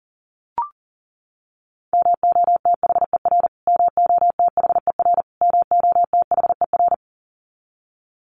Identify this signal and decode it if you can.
Morse code, heard as a keyed tone, about 40 wpm, 700 Hz. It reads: MOTHER MOTHER MOTHER
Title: Top 200 Words Repeated 40wpm